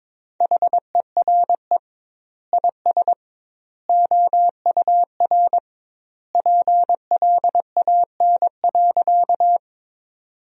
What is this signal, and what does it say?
Morse code, heard as a keyed tone, about 22 wpm, 700 Hz. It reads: HERE IS OUR PLAN.
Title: 2x Word-Spacing - Sentences from Top 500 Words - 22wpm